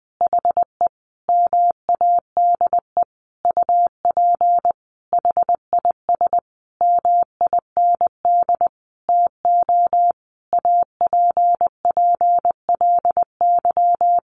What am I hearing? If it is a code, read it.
Morse code, heard as a keyed tone, about 20 wpm, 700 Hz. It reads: HE MADE UP HIS MIND TO APPLY
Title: Sentences from Top 1400 Words 20wpm